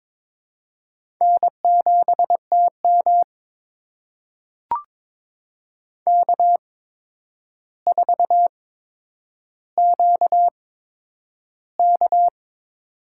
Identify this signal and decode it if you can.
Morse code, heard as a keyed tone, about 22 wpm, spaced as for 9 wpm, 700 Hz. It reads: N7TM K4QK
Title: USA Call Signs - 4 Characters - 8x Character Spacing - 22wpm